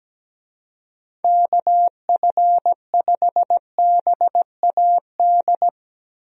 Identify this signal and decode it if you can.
Morse code, heard as a keyed tone, about 17 wpm, 700 Hz. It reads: KF5BAD
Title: USA Call Signs - 6 Characters - Repeated 2x - 17wpm